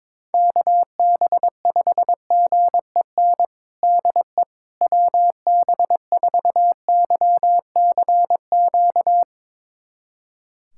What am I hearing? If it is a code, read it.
Morse code, heard as a keyed tone, about 22 wpm, 700 Hz. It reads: KB5GEN DE WB4YCQ
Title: USA Callsign DE Callsign - 6 Characters - 22wpm